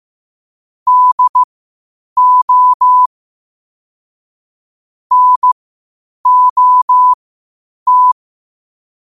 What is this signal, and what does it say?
Morse code, heard as a keyed tone, about 15 wpm, spaced as for 8 wpm, 1000 Hz. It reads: DO NOT